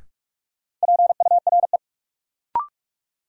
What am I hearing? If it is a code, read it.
Morse code, heard as a keyed tone, about 46 wpm, 700 Hz. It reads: PURE
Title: Sentences from Top 1500 Words - New Word Drill Alt - 45wpm